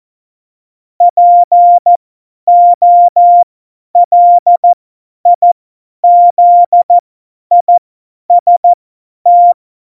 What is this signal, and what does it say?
Morse code, heard as a keyed tone, about 14 wpm, spaced as for 10 wpm, 700 Hz. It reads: POLIZIST
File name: Morsekurs / Witze / Witz 02 Juni 2020, 50 F70